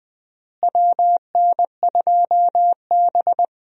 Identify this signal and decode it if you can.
Morse code, heard as a keyed tone, about 20 wpm, 700 Hz. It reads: WN2B
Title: USA Call Signs - 4 Characters - Repeated 3x - 20wpm